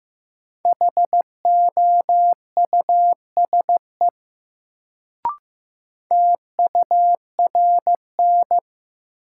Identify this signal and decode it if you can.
Morse code, heard as a keyed tone, about 15 wpm, 700 Hz. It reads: HOUSE TURN
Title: Sentences from Top 200 Words - New Word Drill - 15wpm